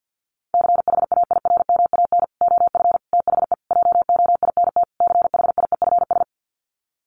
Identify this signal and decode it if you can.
Morse code, heard as a keyed tone, about 48 wpm, 700 Hz. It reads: CHAIRMAN OF THE JOINT CHIEFS
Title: Sets of 5 Words Encore 50wpm